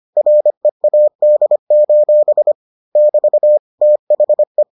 Morse code, heard as a keyed tone, about 25 wpm, 600 Hz. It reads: READ: <BT> THE